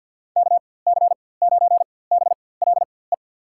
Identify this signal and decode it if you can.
Morse code, heard as a keyed tone, about 48 wpm, spaced as for 23 wpm, 700 Hz. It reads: KC9BLE
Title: USA Call Signs - 6 Characters - 4x Character Spacing - 50wpm